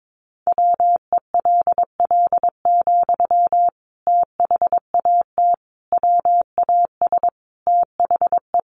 Morse code, heard as a keyed tone, about 22 wpm, 700 Hz. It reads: WELL, THAT WAS THE
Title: Sentences from Top 1200 Words 22wpm